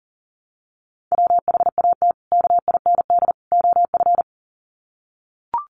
Morse code, heard as a keyed tone, about 40 wpm, 700 Hz. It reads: WHAT KIND OF
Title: Sets of 3 Words 40wpm